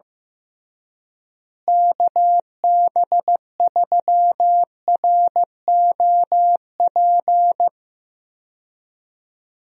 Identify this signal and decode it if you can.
Morse code, heard as a keyed tone, about 15 wpm, 700 Hz. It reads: KB3ROP
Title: USA Call Signs - 6 Characters - Repeated 3x - 15wpm